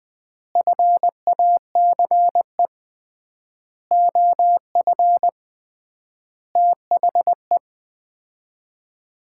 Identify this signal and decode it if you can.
Morse code, heard as a keyed tone, about 21 wpm, 700 Hz. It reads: FACE OF THE
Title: 3x Word-Spacing - Sets of 5 Words - 20wpm